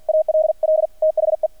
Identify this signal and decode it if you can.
Morse code, heard as a keyed tone, about 48 wpm, 630 Hz. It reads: ALL THE
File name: archived / CRSnet